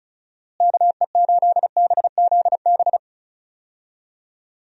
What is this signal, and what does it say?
Morse code, heard as a keyed tone, about 34 wpm, 700 Hz. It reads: KE8BZB
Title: USA Call Signs - 6 Characters - Repeated 3x - 35wpm